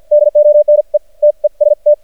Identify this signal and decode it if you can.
Morse code, heard as a keyed tone, about 50 wpm, spaced as for 41 wpm, 600 Hz. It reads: DONE TEST
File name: archived / CRSnet